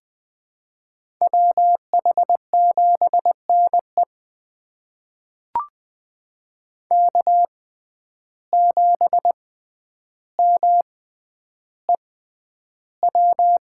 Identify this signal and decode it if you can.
Morse code, heard as a keyed tone, about 20 wpm, spaced as for 10 wpm, 700 Hz. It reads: WH7NE K7MEW
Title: USA Call Signs - 5 Characters - 6x Character Spacing - 20wpm